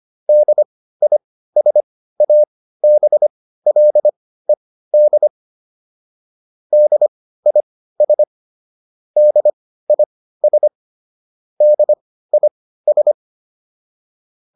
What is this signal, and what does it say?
Morse code, heard as a keyed tone, about 25 wpm, spaced as for 15 wpm, 600 Hz. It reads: DISABLED DIS DIS DIS